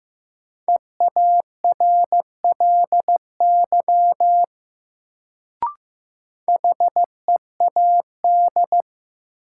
Morse code, heard as a keyed tone, about 15 wpm, 700 Hz. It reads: EARLY HEAD
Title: Sets of 3 Words - New Word Drill - 15wpm